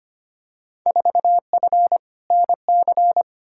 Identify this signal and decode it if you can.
Morse code, heard as a keyed tone, about 25 wpm, 700 Hz. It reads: 4F NC
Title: Field Day Exchanges - 25wpm